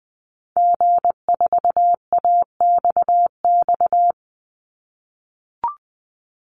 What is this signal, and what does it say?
Morse code, heard as a keyed tone, about 20 wpm, 700 Hz. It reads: G4AXX
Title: Call Signs - 20wpm